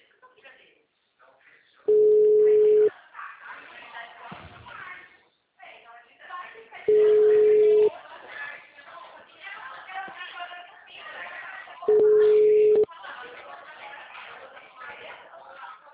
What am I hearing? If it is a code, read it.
Ringback tone (Europe, 425 Hz), with speech.
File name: out- -2001- - - .12627